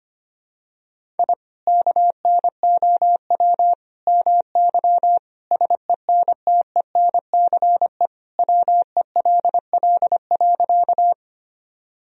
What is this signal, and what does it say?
Morse code, heard as a keyed tone, about 25 wpm, 700 Hz. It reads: I KNOW MY SENTENCE WELL.
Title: Sentences from Top 1000 Words - Review Part 1 - 25wpm